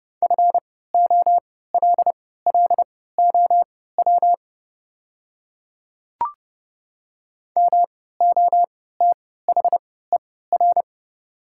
Morse code, heard as a keyed tone, about 30 wpm, spaced as for 17 wpm, 690 Hz. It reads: FOLLOW MOTHER
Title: 3x Character Spacing - Top 200 Words - 30wpm